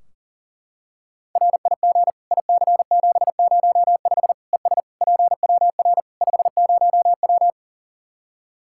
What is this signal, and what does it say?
Morse code, heard as a keyed tone, about 40 wpm, 700 Hz. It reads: RIG IC705 ES PWR 50W
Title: Practice QSO - Exchange 2 - Info - 40wpm